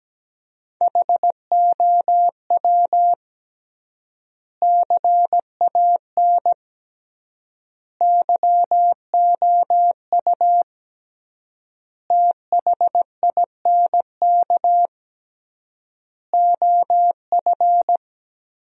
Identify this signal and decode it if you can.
Morse code, heard as a keyed tone, about 17 wpm, 700 Hz. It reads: HOW CAN YOU THINK OF
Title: 3x Word-Spacing - Sentences from Top 200 Words - 17wpm